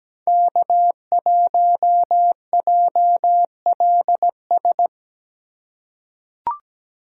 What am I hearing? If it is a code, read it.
Morse code, heard as a keyed tone, about 17 wpm, 700 Hz. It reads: K1JLS
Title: USA Call Signs - 5 Characters - 3x Character Spacing - 17wpm